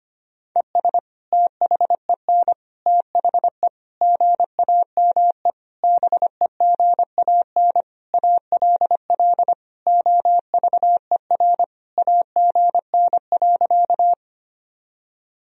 Morse code, heard as a keyed tone, about 25 wpm, 700 Hz. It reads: ES THEN THE GAME BEGAN ALL OVER AGN.
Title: Sentences from Top 500 Words and Abbreviations 25wpm